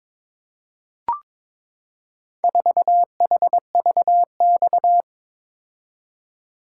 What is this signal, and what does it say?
Morse code, heard as a keyed tone, about 22 wpm, 700 Hz. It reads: 4HVX
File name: practice / 22wpm / Four Letter-Number Combinations - 22wpm